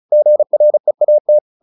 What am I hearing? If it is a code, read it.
Morse code, heard as a keyed tone, about 34 wpm, 600 Hz. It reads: GREAT
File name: the old man and the sea 2 35wpm